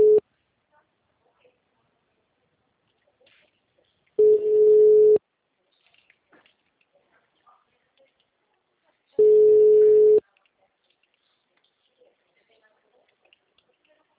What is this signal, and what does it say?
Ringback tone (Europe, 425 Hz), with speech.
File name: out- -2039- - - .36885